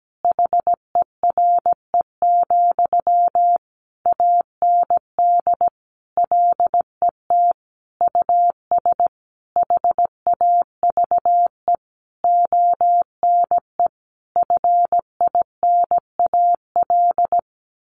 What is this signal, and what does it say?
Morse code, heard as a keyed tone, about 17 wpm, 700 Hz. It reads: HERE, AND LET US HAVE ONE FINAL
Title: Sentences from Top 1000 Words 17wpm